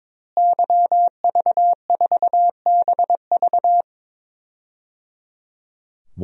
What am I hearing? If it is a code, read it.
Morse code, heard as a keyed tone, about 22 wpm, 700 Hz. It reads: YV4BV
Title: DXCC-100 Call Signs - 5 Characters - Repeated 3x - 22wpm